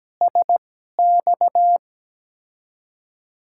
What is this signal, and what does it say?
Morse code, heard as a keyed tone, about 17 wpm, spaced as for 12 wpm, 700 Hz. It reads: SX
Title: USA Call Signs - 6 Characters - 2x Character Spacing - 17wpm